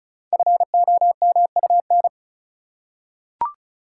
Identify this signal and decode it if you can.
Morse code, heard as a keyed tone, about 36 wpm, 690 Hz. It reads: FOMUN
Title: Code Groups - 5 Letters - 35wpm